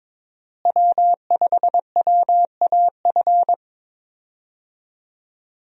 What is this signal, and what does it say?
Morse code, heard as a keyed tone, about 22 wpm, 700 Hz. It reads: W5WAF
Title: USA Call Signs - 5 Characters - Repeated 3x - 22wpm